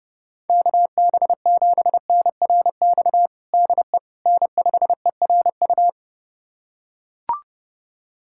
Morse code, heard as a keyed tone, about 30 wpm, 700 Hz. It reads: KB7NRX DE N5ERU
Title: USA Callsign DE Callsign 30wpm